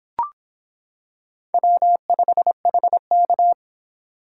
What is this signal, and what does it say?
Morse code, heard as a keyed tone, about 25 wpm, 700 Hz. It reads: W5HK